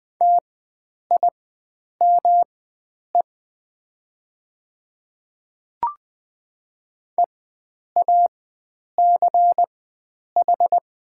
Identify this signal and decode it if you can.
Morse code, heard as a keyed tone, about 20 wpm, spaced as for 9 wpm, 700 Hz. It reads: TIME EACH
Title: 4x Character Spacing - Top 100 Words - 20wpm